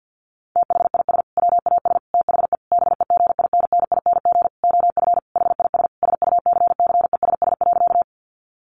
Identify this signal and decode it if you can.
Morse code, heard as a keyed tone, about 50 wpm, 700 Hz. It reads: THIS WAS THE BEGINNING OF HIS SUCCESS.
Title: Sentences from Top 900 Words 50wpm